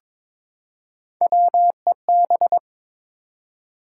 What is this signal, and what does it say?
Morse code, heard as a keyed tone, about 23 wpm, 700 Hz. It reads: WEB
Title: Three Letter Words - 22wpm